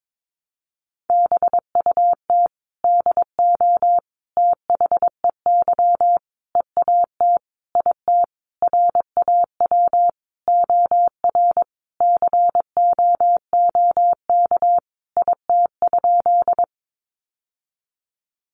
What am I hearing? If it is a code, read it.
Morse code, heard as a keyed tone, about 22 wpm, 700 Hz. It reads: BUT DO THEY EAT IT RAW OR COOK IT?
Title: Sentences from Top 1500 Words 22wpm